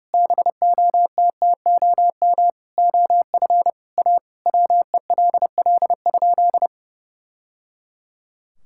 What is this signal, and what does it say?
Morse code, heard as a keyed tone, about 30 wpm, 700 Hz. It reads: BOTTOM OF A WELL?
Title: Sentences from Top 1000 Words - Review Part 6 - 30wpm